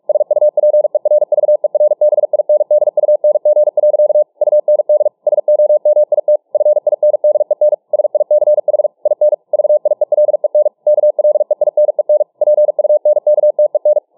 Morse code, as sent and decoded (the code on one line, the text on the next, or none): ... ..- .--. .-. ...- . .-. -... .. -. -.. ..- -. --. .-.-.- / ..- -. -.. / ... --- -- .. - / ..-. .. -. -.. -. / ... .. -.-. .... / .. -. / ...- .. .-.. -. / -.- .-.. .. -. -. / .--. ..- -. -.- - . -.
SUPERVERBINDUNG. UND SOMIT FINDEN SICH IN VIELEN KLEINEN PUNKTEN